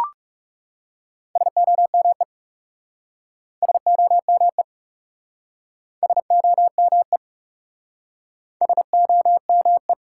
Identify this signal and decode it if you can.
Morse code, heard as a keyed tone, about 40 wpm, 700 Hz. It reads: SOME SOME SOME SOME